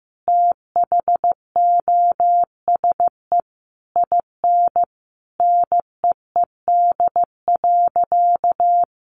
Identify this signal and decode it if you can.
Morse code, heard as a keyed tone, about 15 wpm, 700 Hz. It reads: THOSE IN NEED.
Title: Sentences from Top 1200 Words 15wpm